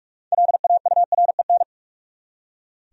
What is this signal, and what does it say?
Morse code, heard as a keyed tone, about 42 wpm, 700 Hz. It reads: LAUREN
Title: USA Common Names - 45wpm